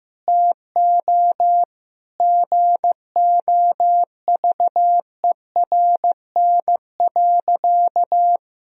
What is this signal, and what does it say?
Morse code, heard as a keyed tone, about 15 wpm, 700 Hz. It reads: TO GOVERN.
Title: Sentences from Top 1000 Words - Review Part 2 - 15wpm